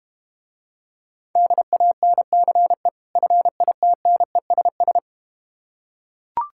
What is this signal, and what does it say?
Morse code, heard as a keyed tone, about 32 wpm, 700 Hz. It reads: DANCE FITNESS